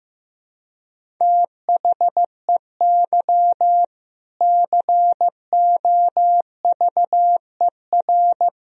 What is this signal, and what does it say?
Morse code, heard as a keyed tone, about 15 wpm, 700 Hz. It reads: THEY COVER
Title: Sentences from Top 1000 Words - Review Part 1 - 15wpm